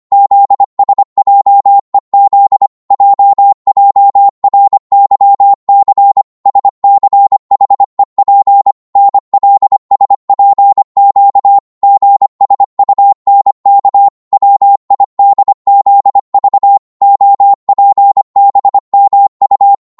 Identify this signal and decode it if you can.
Morse code, heard as a keyed tone, about 25 wpm, 800 Hz. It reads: ZSJEZ JJRYC SCHEP NLSPQ GSUNK WIDZV OPBMU